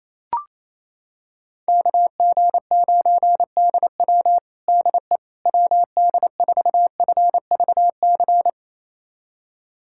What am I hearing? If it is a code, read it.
Morse code, heard as a keyed tone, about 28 wpm, 700 Hz. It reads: KG9DW DE WD4FVC
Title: USA Callsign DE Callsign 28wpm